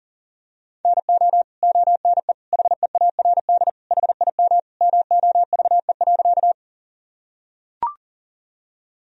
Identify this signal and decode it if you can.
Morse code, heard as a keyed tone, about 40 wpm, 700 Hz. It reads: NO ONE HEARD HIM MOVE.
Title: Sentences from Top 1000 Words - Review Part 1 - 40wpm